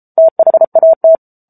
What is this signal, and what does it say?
Morse code, heard as a keyed tone, about 34 wpm, 650 Hz. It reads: THAT
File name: Book Two / 07-Chapter 7 - 33 WPM